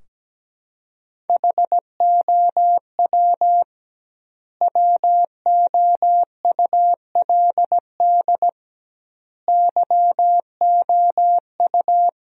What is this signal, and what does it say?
Morse code, heard as a keyed tone, about 17 wpm, 700 Hz. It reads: HOW WOULD YOU
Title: 2x Word-Spacing - Sentences from Top 100 Words - 17wpm